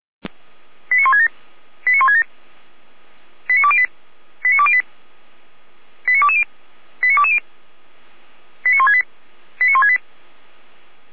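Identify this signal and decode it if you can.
Five-tone selective call (ZVEI), heard as quick runs of five tones, tones of about 70 ms. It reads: 89167 89167 89208 89208 89220 89220 89168 89168
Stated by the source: ZVEI five-tone selective call received on an SDR.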